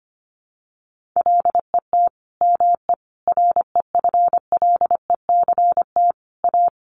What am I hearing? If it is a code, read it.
Morse code, heard as a keyed tone, about 25 wpm, 700 Hz. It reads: LET ME REFLECT A
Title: Sentences from Top 1500 Words 25wpm